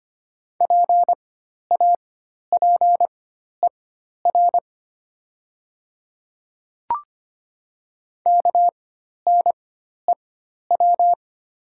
Morse code, heard as a keyed tone, about 25 wpm, spaced as for 12 wpm, 700 Hz. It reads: PAPER KNEW
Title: 4x Character Spacing - Top 500 Words - 25wpm